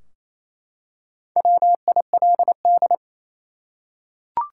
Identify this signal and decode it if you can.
Morse code, heard as a keyed tone, about 27 wpm, 700 Hz. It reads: WILD
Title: Four Letter Words - 28wpm